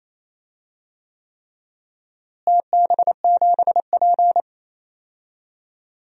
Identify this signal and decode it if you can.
Morse code, heard as a keyed tone, about 28 wpm, 700 Hz. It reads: TB7P
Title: DXCC-100 Call Signs - 4 Characters - Repeated 3x - 28wpm